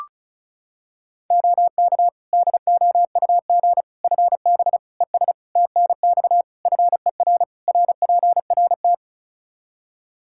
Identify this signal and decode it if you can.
Morse code, heard as a keyed tone, about 36 wpm, 700 Hz. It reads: OK DOUG FB ES TNX FER RPRT
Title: Practice QSO - Exchange 2 - Info - 35wpm